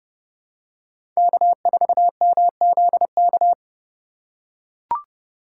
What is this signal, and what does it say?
Morse code, heard as a keyed tone, about 32 wpm, 700 Hz. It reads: K4MZK
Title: USA Call Signs - 5 Characters - 3x Character Spacing - 30wpm